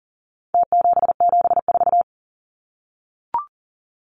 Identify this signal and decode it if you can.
Morse code, heard as a keyed tone, about 40 wpm, 700 Hz. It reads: T774
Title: Numbers - Contest Style - 4-Digit 40wpm